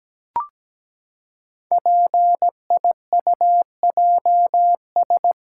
Morse code, heard as a keyed tone, about 17 wpm, 700 Hz. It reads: PIUJS
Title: Code Groups - 5 Letters - 17wpm